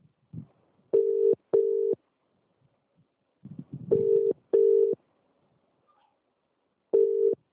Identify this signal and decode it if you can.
Ringback tone (UK), 400 + 450 Hz.